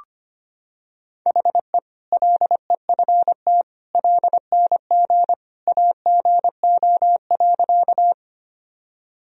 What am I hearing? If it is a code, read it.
Morse code, heard as a keyed tone, about 25 wpm, 700 Hz. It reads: HE LEFT LNG AGO.
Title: Sentences from Top 500 Words and Abbreviations 25wpm